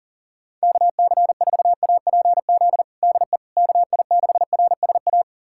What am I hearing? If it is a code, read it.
Morse code, heard as a keyed tone, about 40 wpm, 700 Hz. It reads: KC4APZ DE KI6RSA
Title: USA Callsign DE Callsign - 6 Characters - 40wpm